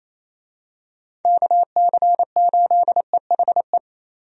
Morse code, heard as a keyed tone, about 28 wpm, 700 Hz. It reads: KC8EHE
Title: USA Call Signs - 6 Characters - Repeated 3x - 28wpm